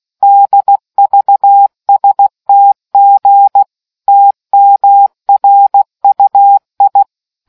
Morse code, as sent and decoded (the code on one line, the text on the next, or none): -.. ...- ... - --. / - -- .-. ..- ..
DVSTG TMRUI